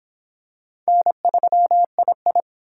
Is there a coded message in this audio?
Yes: Morse code from 0.9 to 2.4 s, heard as a keyed tone, about 25 wpm, 700 Hz. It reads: N3II